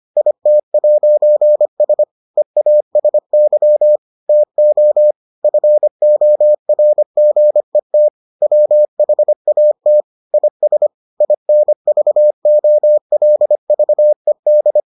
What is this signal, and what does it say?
Morse code, heard as a keyed tone, about 25 wpm, 600 Hz. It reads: IT'S EASY TO FORGET WHAT IS INVOLVED